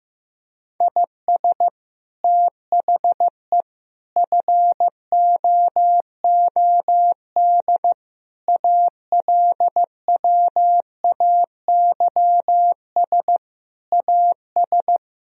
Morse code, heard as a keyed tone, about 15 wpm, 700 Hz. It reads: IS THE FOOD ALWAYS AS
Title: Sentences from Top 1000 Words - Review Part 1 - 15wpm